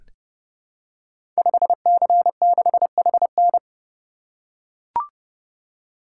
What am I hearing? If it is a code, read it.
Morse code, heard as a keyed tone, about 30 wpm, 700 Hz. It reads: 5C6HN
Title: Code Groups - 5 Letter:Numbers - 30wpm